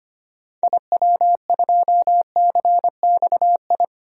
Morse code, heard as a keyed tone, about 25 wpm, 700 Hz. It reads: IW2CXI
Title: DXCC-100 Call Signs - 6 Characters - 25wpm